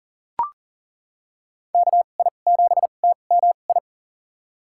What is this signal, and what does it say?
Morse code, heard as a keyed tone, about 40 wpm, spaced as for 29 wpm, 700 Hz. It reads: KI7TMI
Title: USA Call Signs - 6 Characters - 2x Character Spacing - 40wpm